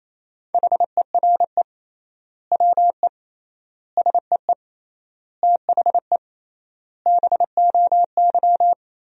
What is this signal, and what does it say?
Morse code, heard as a keyed tone, about 28 wpm, 700 Hz. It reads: HERE WE SEE THE BOY
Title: 3x Word-Spacing - Sentences from Top 300 Words - 28wpm